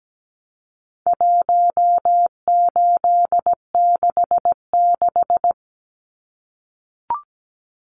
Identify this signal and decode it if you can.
Morse code, heard as a keyed tone, about 17 wpm, 700 Hz. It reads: 1866
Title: Numbers - 4-Digit 17wpm